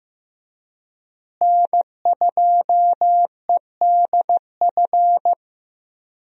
Morse code, heard as a keyed tone, about 15 wpm, 700 Hz. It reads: N2EDF
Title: USA Call Signs - 5 Characters - Repeated 3x - 15wpm